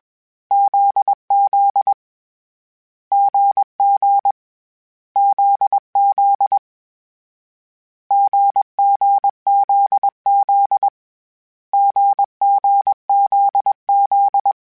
Morse code, heard as a keyed tone, about 21 wpm, 800 Hz. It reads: ZZ GG ZZ GGZZ GGZZ
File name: Difficult Combos / Difficult Combos-mp3s / GZ GZ7 @20